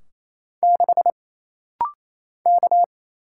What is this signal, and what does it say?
Morse code, heard as a keyed tone, about 28 wpm, 700 Hz. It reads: 6 K
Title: Single Letter-Number Alt - Rapid-Fire 28wpm